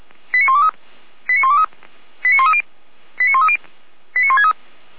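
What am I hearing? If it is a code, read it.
Five-tone selective call (ZVEI), heard as quick runs of five tones, tones of about 70 ms. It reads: 89123 89123 89130 89130 89163
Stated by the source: ZVEI five-tone selective call 89119 received on SDR.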